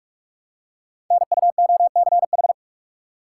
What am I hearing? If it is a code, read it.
Morse code, heard as a keyed tone, about 44 wpm, 700 Hz. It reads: NUOCH
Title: Code Groups - 5 Letters - 45wpm